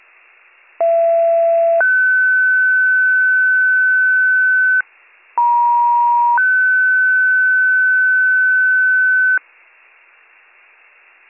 Two-tone page, heard as a one-second tone, then a three-second tone, about 670 then 1500 Hz, about 950 then 1500 Hz. It is heard from 0.8 to 9.4 s.